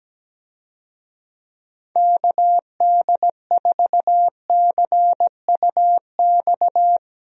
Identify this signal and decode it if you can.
Morse code, heard as a keyed tone, about 17 wpm, 700 Hz. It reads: KD4CUX
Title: USA Call Signs - 6 Characters - Repeated 2x - 17wpm